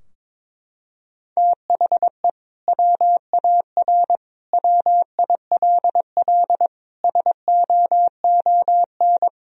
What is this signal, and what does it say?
Morse code, heard as a keyed tone, about 22 wpm, 700 Hz. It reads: THE WAR WILL SOON
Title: Sentences from Top 1000 Words - Review Part 2 - 22wpm